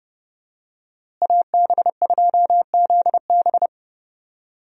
Morse code, heard as a keyed tone, about 32 wpm, 700 Hz. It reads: AB2ZB
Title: USA Call Signs - 5 Characters - Repeated 3x - 30wpm